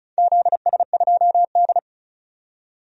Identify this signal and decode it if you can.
Morse code, heard as a keyed tone, about 36 wpm, 700 Hz. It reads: ZS2D
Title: DXCC-100 Call Signs - 4 Characters - 35wpm